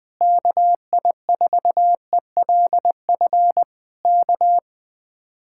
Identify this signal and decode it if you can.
Morse code, heard as a keyed tone, about 21 wpm, 700 Hz. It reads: KI4ELF K
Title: Speed-Racing Common QSO Phrases - 20wpm